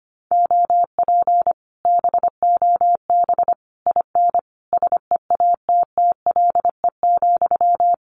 Morse code, heard as a keyed tone, about 25 wpm, 700 Hz. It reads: OP BOB IN SEATTLE,
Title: Common QSO Phrases Encore 25wpm